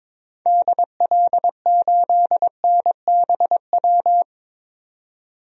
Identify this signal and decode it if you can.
Morse code, heard as a keyed tone, about 21 wpm, 700 Hz. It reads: DL8NBW
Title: DXCC-100 Call Signs - 6 Characters - 22wpm